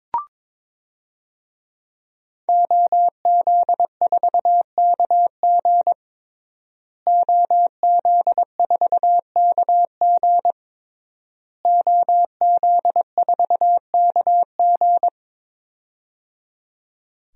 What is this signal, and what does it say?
Morse code, heard as a keyed tone, about 22 wpm, 700 Hz. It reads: OZ4KG OZ4KG OZ4KG
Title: DXCC-100 Call Signs - 5 Characters - Repeated 3x - 22wpm